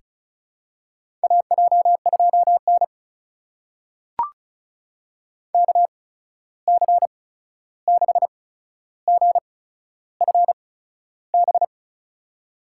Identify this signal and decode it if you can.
Morse code, heard as a keyed tone, about 36 wpm, spaced as for 13 wpm, 700 Hz. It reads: AJ2N KC6GFB
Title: USA Call Signs - 8x Character Spacing - 35wpm